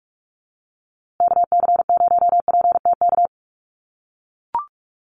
Morse code, heard as a keyed tone, about 44 wpm, 700 Hz. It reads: KC0PTK
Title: USA Call Signs Repeated 2x 45wpm